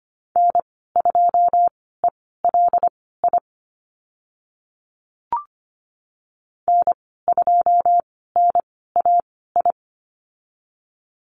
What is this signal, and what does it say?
Morse code, heard as a keyed tone, about 25 wpm, spaced as for 16 wpm, 690 Hz. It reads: N2ELI N2NAI
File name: Lesson 15g - Number 5 - Call Signs 25wpm